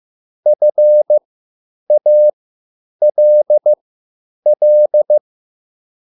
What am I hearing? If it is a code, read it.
Morse code, heard as a keyed tone, about 15 wpm, spaced as for 9 wpm, 600 Hz. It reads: FALL